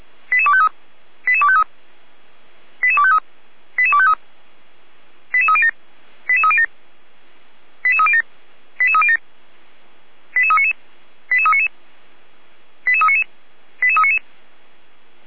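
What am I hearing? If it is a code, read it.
Five-tone selective call (ZVEI), heard as quick runs of five tones, tones of about 70 ms. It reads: 88253 88253 88253 88253 88307 88307 88307 88307 88399 88399 88399 88399
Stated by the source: ZVEI five-tone selective call received on an SDR, fire brigade Feuerwehr Orlamünde.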